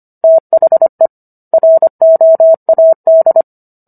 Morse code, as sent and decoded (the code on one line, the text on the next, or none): - .... / .-. --- .- -..
THE ROAD